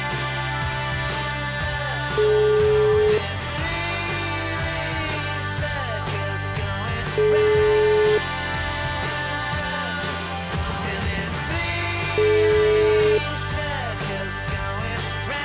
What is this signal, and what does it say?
Ringback tone (Europe, 425 Hz), with noise.